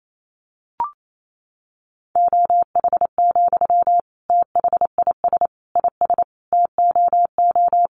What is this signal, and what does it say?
Morse code, heard as a keyed tone, about 28 wpm, 700 Hz. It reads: OH, THIS IS TOO
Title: Sentences from Top 500 Words 28wpm